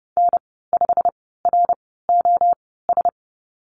Morse code, heard as a keyed tone, about 30 wpm, spaced as for 17 wpm, 690 Hz. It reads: N5ROS
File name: Lesson 15g - Number 5 - Call Signs 30wpm